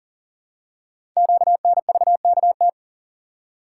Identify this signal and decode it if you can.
Morse code, heard as a keyed tone, about 40 wpm, 700 Hz. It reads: QNVKT
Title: Code Groups - 5 Letter:Numbers - 40wpm